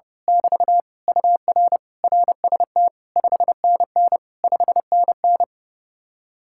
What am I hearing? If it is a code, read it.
Morse code, heard as a keyed tone, about 28 wpm, 700 Hz. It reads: <BT> UR RST 5NN 5NN
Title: Speed-Racing Common QSO Phrases - 30wpm